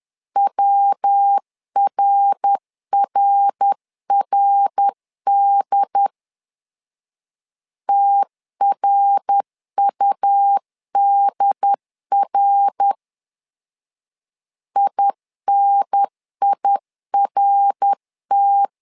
Morse code, heard as a keyed tone, about 10 wpm, 800 Hz. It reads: WRRRD TRUDR INIRT
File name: LEZIONI / 3 MWRO / 3 MWR 10wpm 1952